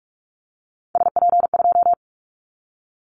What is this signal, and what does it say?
Morse code, heard as a keyed tone, about 44 wpm, 700 Hz. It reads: SP2
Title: Callsign Prefixes for Top 25 DXCC - Repeated 3x - 45wpm